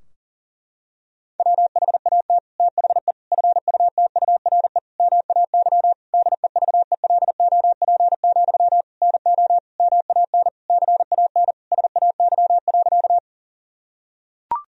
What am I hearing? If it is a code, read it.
Morse code, heard as a keyed tone, about 40 wpm, 700 Hz. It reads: WHAT THE FUTURE MAY DEVELOP, NO MAN CAN SAY.
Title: Sentences from Top 1000 Words - Review Part 2 - 40wpm